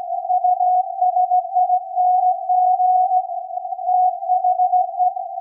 Morse code, heard as a keyed tone, about 17 wpm, 750 Hz. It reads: USING THE
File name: iz7slz / VLF / SAQ 24-10-2023